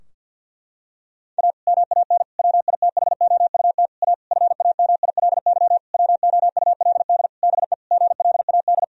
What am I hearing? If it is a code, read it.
Morse code, heard as a keyed tone, about 50 wpm, 700 Hz. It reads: A MAN WITHOUT A FAMILY WOULD BE GLAD
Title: Sentences from Top 1000 Words - Review Part 3 - 50wpm